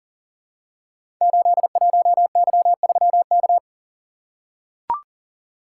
Morse code, heard as a keyed tone, about 39 wpm, 700 Hz. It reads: 81Y3K
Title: Code Groups - 5 Letter:Numbers - 40wpm